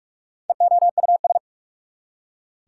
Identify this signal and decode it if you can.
Morse code, heard as a keyed tone, about 44 wpm, 690 Hz. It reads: EOUS
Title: Prefixes and Suffixes 45wpm